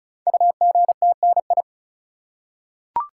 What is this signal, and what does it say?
Morse code, heard as a keyed tone, about 36 wpm, 700 Hz. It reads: UGTNI
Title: Code Groups - 5 Letters - 35wpm